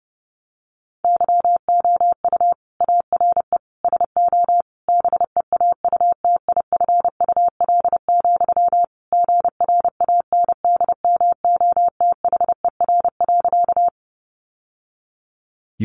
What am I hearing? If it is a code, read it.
Morse code, heard as a keyed tone, about 30 wpm, 700 Hz. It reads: YOU ARE SO BEAUTIFUL, GRANDMOTHER.
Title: Sentences from Top 1200 Words 30wpm